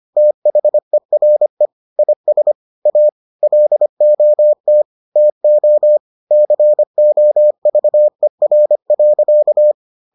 Morse code, heard as a keyed tone, about 25 wpm, 600 Hz. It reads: THERE IS A LOT TO COVER.